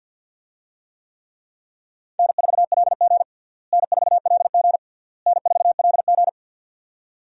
Morse code, heard as a keyed tone, about 50 wpm, 700 Hz. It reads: N4LG N4LG N4LG
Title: USA Call Signs - 4 Characters - Repeated 3x - 50wpm